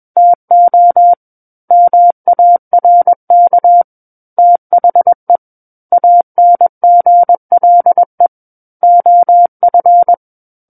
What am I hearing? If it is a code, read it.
Morse code, heard as a keyed tone, about 21 wpm, 700 Hz. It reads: TO MARK THE ANGLE OF
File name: Book One / 12-Chapter 12 - 21 WPM